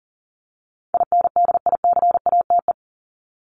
Morse code, heard as a keyed tone, about 40 wpm, 700 Hz. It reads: INDICATE
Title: Top 1251-1500 Words 40wpm